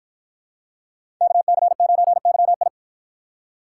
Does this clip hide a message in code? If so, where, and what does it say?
Yes: Morse code from 1.2 to 2.7 s, heard as a keyed tone, about 50 wpm, 700 Hz. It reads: KC9YI